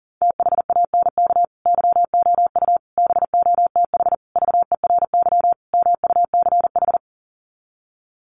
Morse code, heard as a keyed tone, about 40 wpm, 700 Hz. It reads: THANK YOU BOTH VERY MUCH
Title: Sets of 5 Words 40wpm